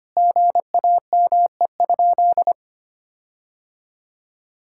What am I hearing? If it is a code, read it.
Morse code, heard as a keyed tone, about 25 wpm, 700 Hz. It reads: GAME?
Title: Sentences from Top 1600 Words 25wpm